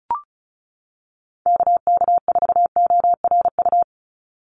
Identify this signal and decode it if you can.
Morse code, heard as a keyed tone, about 34 wpm, 700 Hz. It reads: KK4ORU
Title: USA Call Signs 35wpm